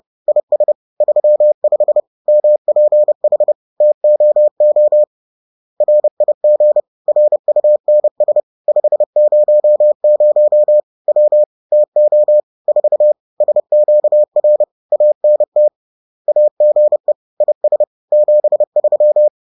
Morse code, heard as a keyed tone, about 30 wpm, 600 Hz. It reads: IS 35 MPH TOO RIG RUNS 500 W TO 4 SQR ANT AGE IS 73
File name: 30wpm